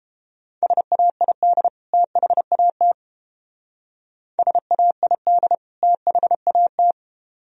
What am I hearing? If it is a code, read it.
Morse code, heard as a keyed tone, about 32 wpm, 700 Hz. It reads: SAID THAT SAID THAT